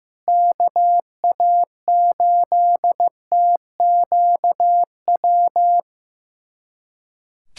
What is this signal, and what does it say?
Morse code, heard as a keyed tone, about 16 wpm, 700 Hz. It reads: KA8TQW
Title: USA Callsign DE Callsign 15wpm